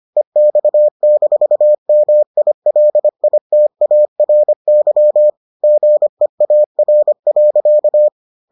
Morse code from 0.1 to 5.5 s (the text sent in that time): EX-MILITARY